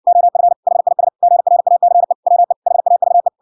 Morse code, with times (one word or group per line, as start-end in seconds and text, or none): 0.1-0.5 OF
0.7-1.1 HIS
1.2-2.1 GRACE
2.3-2.5 WE
2.7-3.3 HAVE